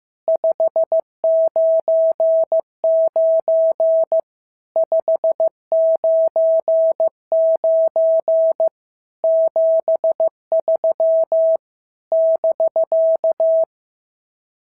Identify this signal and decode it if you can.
Morse code, heard as a keyed tone, about 15 wpm, 650 Hz. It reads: 599 599 73 <BK>
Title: SOTA QSO Practice 15wpm